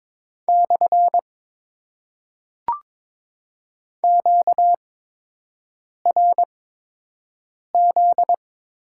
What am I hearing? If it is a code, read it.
Morse code, heard as a keyed tone, about 22 wpm, spaced as for 6 wpm, 700 Hz. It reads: / QRZ
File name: Top 90 QSO Elements - 8x Character Spacing - 22wpm